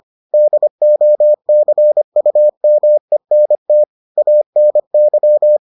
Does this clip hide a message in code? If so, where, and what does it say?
Yes: Morse code from 0.3 to 5.6 s, heard as a keyed tone, about 25 wpm, 600 Hz. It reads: DOCUMENT ANY